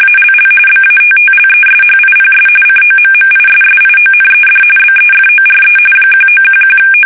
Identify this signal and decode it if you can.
RTTY, about 45.45 baud. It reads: ALL BATTLESHIPS COMMISSIONED BEFORE 1911
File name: fsk850 / battleshipnj